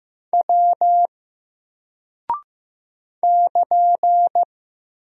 Morse code, heard as a keyed tone, about 15 wpm, 700 Hz. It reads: W <KN>
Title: Single Character (Letter:Number:Punctuation:Prosign) 15wpm